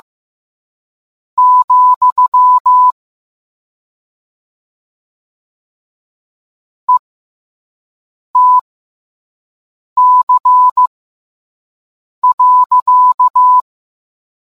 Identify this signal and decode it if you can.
Morse code, heard as a keyed tone, about 15 wpm, spaced as for 5 wpm, 1000 Hz. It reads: , ETC.